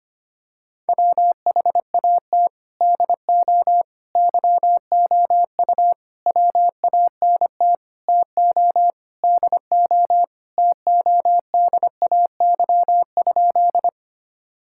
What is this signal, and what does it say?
Morse code, heard as a keyed tone, about 25 wpm, 700 Hz. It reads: WHAT DO YOU WANT TO DO TODAY?
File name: practice / 25wpm / Sentences with 4-Word Phrases - 25wpm